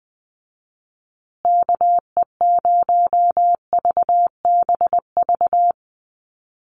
Morse code, heard as a keyed tone, about 20 wpm, 700 Hz. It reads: KE0VBV
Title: USA Call Signs Repeated 3x 20wpm